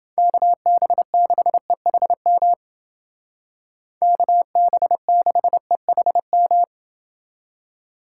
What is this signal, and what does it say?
Morse code, heard as a keyed tone, about 28 wpm, 700 Hz. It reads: KB6EHM KB6EHM